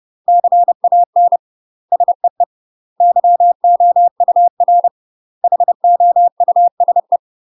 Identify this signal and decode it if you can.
Morse code, heard as a keyed tone, about 30 wpm, 700 Hz. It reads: CAN SEE YOUR HOUSE
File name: UK MASTER QSOs TRAINING / 30WPM / UK MASTER QSOs 0030